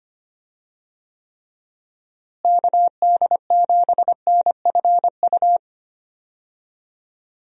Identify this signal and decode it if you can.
Morse code, heard as a keyed tone, about 25 wpm, 700 Hz. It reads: KD7NFU
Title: USA Call Signs - 6 Characters - Repeated 2x - 25wpm